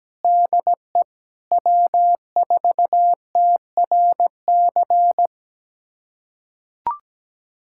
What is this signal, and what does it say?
Morse code, heard as a keyed tone, about 17 wpm, 700 Hz. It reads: DE W4TRC
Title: USA Callsign DE Callsign - 5 Characters - 17wpm